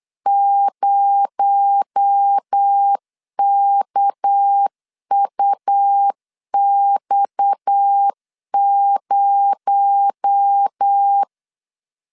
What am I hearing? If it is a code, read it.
Morse code, heard as a keyed tone, about 8 wpm, 800 Hz. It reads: 0KUX0
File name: LEZIONI / 6 PKX0 / 6 PKX0 8wpm 3341